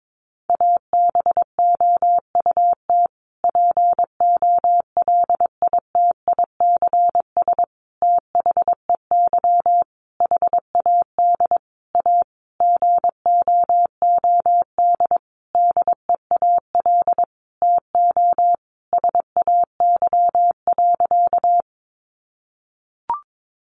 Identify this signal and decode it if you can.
Morse code, heard as a keyed tone, about 22 wpm, 700 Hz. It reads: ABOUT POLITICS THEY HAD A GOOD DEAL TO SAY.
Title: Sentences from Top 1000 Words 22wpm